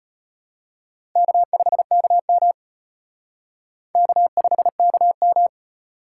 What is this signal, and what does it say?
Morse code, heard as a keyed tone, about 34 wpm, 700 Hz. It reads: K5KM K5KM